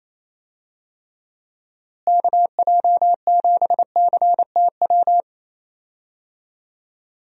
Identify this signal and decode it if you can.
Morse code, heard as a keyed tone, about 28 wpm, 700 Hz. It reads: KJ7CTW
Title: USA Call Signs - 6 Characters - Repeated 3x - 28wpm